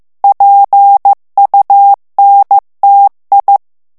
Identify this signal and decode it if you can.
Morse code, heard as a keyed tone, about 15 wpm, 800 Hz. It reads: PUNTI
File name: morse mp3 / ITALIEN texte clair 2 vitesse 14 mots minute